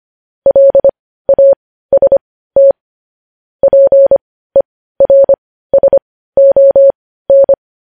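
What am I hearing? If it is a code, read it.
Morse code, heard as a keyed tone, about 25 wpm, spaced as for 15 wpm, 550 Hz. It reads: LAST PERSON
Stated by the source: News headlines sent in Morse code at 15 wpm.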